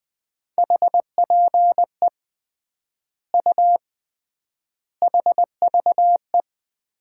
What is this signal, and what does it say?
Morse code, heard as a keyed tone, about 20 wpm, 700 Hz. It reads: HPE U HVE
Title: Common QSO Phrases - 3x Word-Spacing - 20wpm